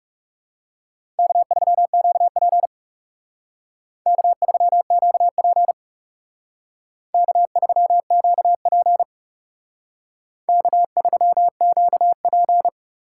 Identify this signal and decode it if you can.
Morse code, heard as a keyed tone, about 36 wpm, 700 Hz. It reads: K3QP K3QP K3QP K3QP
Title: Speed Racing - USA 1x2 Call Signs 30wpm